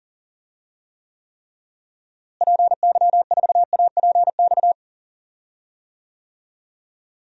Morse code, heard as a keyed tone, about 37 wpm, 700 Hz. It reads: PY4APX
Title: DXCC-100 Call Signs - 6 Characters - Repeated 3x - 40wpm